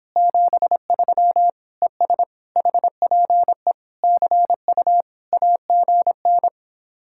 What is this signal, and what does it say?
Morse code, heard as a keyed tone, about 31 wpm, 700 Hz. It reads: 73 ES HPE CU AGN